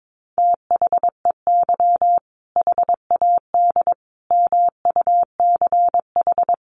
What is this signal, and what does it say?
Morse code, heard as a keyed tone, about 22 wpm, 700 Hz. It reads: THEY HAD MUCH
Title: Sentences from Top 400 Words 22wpm